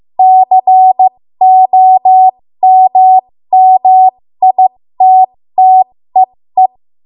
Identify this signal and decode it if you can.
Morse code, heard as a keyed tone, about 15 wpm, spaced as for 13 wpm, 750 Hz. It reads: COMMITTEE
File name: Morse / 13WPM